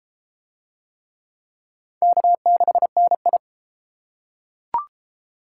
Morse code, heard as a keyed tone, about 29 wpm, 700 Hz. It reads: K6NI